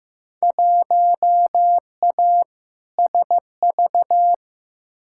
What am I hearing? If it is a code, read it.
Morse code, heard as a keyed tone, about 15 wpm, 700 Hz. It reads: 1A SV
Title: Field Day Exchanges Encore - 15wpm